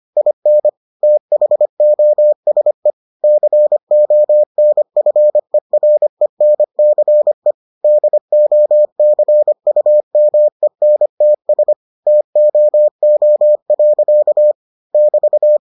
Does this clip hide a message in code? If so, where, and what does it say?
Yes: Morse code from 0.2 to 15.6 s, heard as a keyed tone, about 25 wpm, 600 Hz. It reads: IN THOSE CONFERENCE DOCUMENTS TOO. <BT>